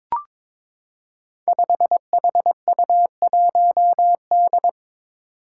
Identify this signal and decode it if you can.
Morse code, heard as a keyed tone, about 22 wpm, 700 Hz. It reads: 5HU1D
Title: Five Challenging Letter-Number Combinations 22wpm